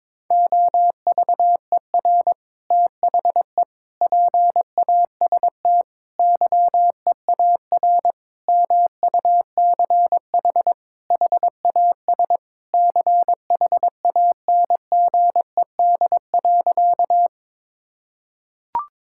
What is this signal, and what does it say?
Morse code, heard as a keyed tone, about 22 wpm, 700 Hz. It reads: OVER THE PAST YEAR MUCH HAS CHANGED.
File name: practice / 22wpm / Sentences with 3-Word Phrases - 22wpm